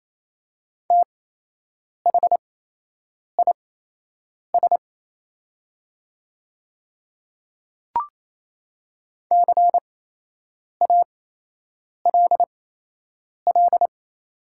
Morse code, heard as a keyed tone, about 28 wpm, spaced as for 8 wpm, 700 Hz. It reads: THIS CALL
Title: 8x Character Spacing - Top 100 Words - 28wpm